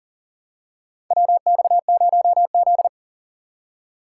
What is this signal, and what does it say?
Morse code, heard as a keyed tone, about 40 wpm, 700 Hz. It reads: WX0Z
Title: K1USN SST QSO Practice - 40wpm